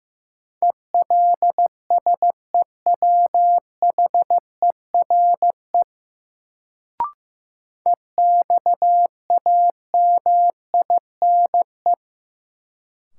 Morse code, heard as a keyed tone, about 15 wpm, 700 Hz. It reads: ELSEWHERE EXAMINE
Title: Sentences from Top 1600 Words - New Word Drill - 15wpm